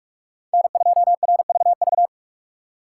Morse code, heard as a keyed tone, about 44 wpm, 700 Hz. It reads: N2RVV